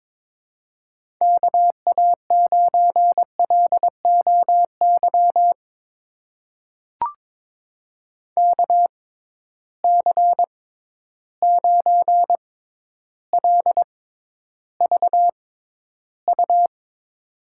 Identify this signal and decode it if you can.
Morse code, heard as a keyed tone, about 22 wpm, spaced as for 11 wpm, 700 Hz. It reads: KA9LOY KC9LVU
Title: USA Call Signs - 6 Characters - 6x Character Spacing - 22wpm